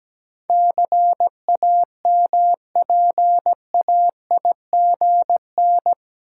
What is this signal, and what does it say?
Morse code, heard as a keyed tone, about 17 wpm, 700 Hz. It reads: CAMPAIGN
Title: Sentences from Top 1200 Words - New Word Drill Alt - 17wpm